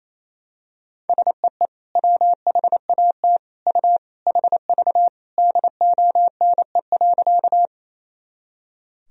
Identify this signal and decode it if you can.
Morse code, heard as a keyed tone, about 28 wpm, 700 Hz. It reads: SEE WHAT U HV DONE.
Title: Sentences from Top 500 Words and Abbreviations 28wpm